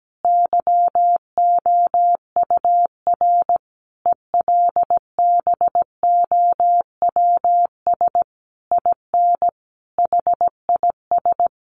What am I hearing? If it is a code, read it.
Morse code, heard as a keyed tone, about 17 wpm, 700 Hz. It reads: YOUR ELBOWS IN HIS
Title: A Sound Of Thunder 17wpm